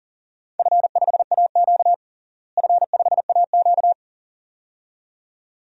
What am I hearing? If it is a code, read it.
Morse code, heard as a keyed tone, about 40 wpm, 700 Hz. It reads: F5AQ F5AQ
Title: DXCC-100 Call Signs - 4 Characters - Repeated 2x - 40wpm